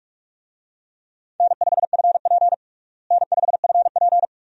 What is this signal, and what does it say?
Morse code, heard as a keyed tone, about 46 wpm, 700 Hz. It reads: N5FP N5FP
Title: USA Call Signs - 4 Characters - Repeated 2x - 45wpm